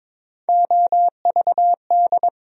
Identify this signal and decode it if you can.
Morse code, heard as a keyed tone, about 22 wpm, 700 Hz. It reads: OVD
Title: USA Call Sign Postfixes - 3 Letters 22wpm